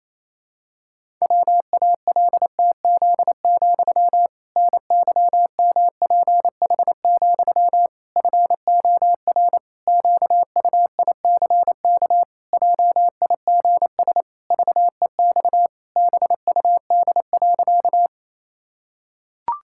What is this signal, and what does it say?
Morse code, heard as a keyed tone, about 28 wpm, 700 Hz. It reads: WALTZ, NYMPH, FOR QUICK JIGS VEX BUD.
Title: Pangram Challenge v2 - 28wpm